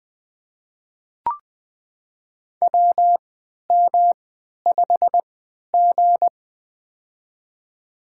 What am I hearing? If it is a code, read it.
Morse code, heard as a keyed tone, about 20 wpm, spaced as for 11 wpm, 700 Hz. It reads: WM5G
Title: USA Call Signs - 4 Characters - 3x Character Spacing - 20wpm